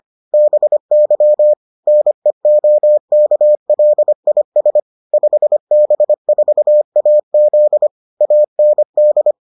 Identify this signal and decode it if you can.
Morse code, heard as a keyed tone, about 25 wpm, 600 Hz. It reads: BY NEOKLIS 5B4AZ AND